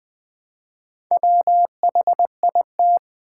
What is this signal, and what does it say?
Morse code, heard as a keyed tone, about 20 wpm, 700 Hz. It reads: WHIT
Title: Build A Word - 1 Letter At A Time - 5-Letter Words - 20wpm